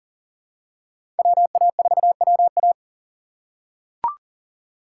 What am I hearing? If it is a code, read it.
Morse code, heard as a keyed tone, about 40 wpm, 700 Hz. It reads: WA4WA
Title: USA Call Signs - 5 Characters - 2x Character Spacing - 40wpm